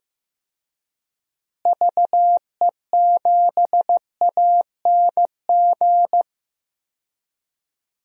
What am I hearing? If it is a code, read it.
Morse code, heard as a keyed tone, about 15 wpm, 700 Hz. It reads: VE7ANG
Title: DXCC-100 Call Signs - 6 Characters - Repeated 3x - 15wpm